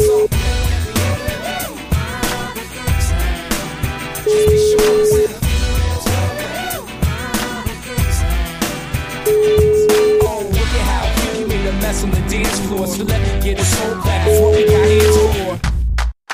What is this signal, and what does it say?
Ringback tone (Europe, 425 Hz), with noise.